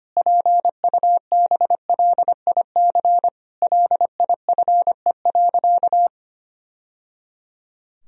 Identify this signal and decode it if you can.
Morse code, heard as a keyed tone, about 25 wpm, 700 Hz. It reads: PUBLIC LIFE.
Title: Sentences from Top 1000 Words - Review Part 3 - 25wpm